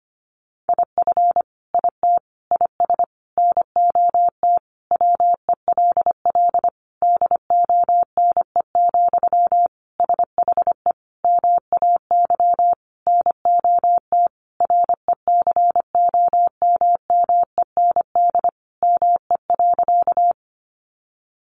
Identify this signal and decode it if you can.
Morse code, heard as a keyed tone, about 25 wpm, 700 Hz. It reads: IF IT IS NOT WELL DONE, SHE MAY NOT RECOMMEND ME.
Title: Sentences from Top 1200 Words 25wpm